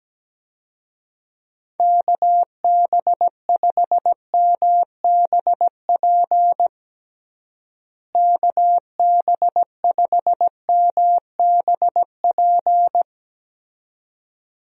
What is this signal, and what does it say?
Morse code, heard as a keyed tone, about 17 wpm, 700 Hz. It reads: KB5MBP KB5MBP
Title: USA Call Signs - 6 Characters - Repeated 2x - 17wpm